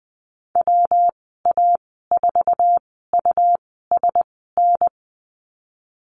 Morse code, heard as a keyed tone, about 19 wpm, spaced as for 14 wpm, 700 Hz. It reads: WA4USN
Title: 167 Lesson 27g - Number 7 - Call Signs 20wpm